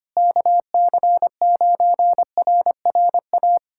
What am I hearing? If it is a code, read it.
Morse code, heard as a keyed tone, about 25 wpm, 700 Hz. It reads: KC9RRA
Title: USA Callsign DE Callsign 25wpm